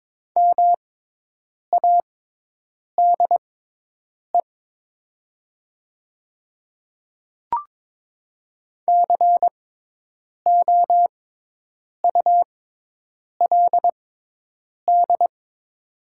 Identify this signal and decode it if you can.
Morse code, heard as a keyed tone, about 22 wpm, spaced as for 8 wpm, 700 Hz. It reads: MADE COULD
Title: 6x Character Spacing - Top 300 Words - 22wpm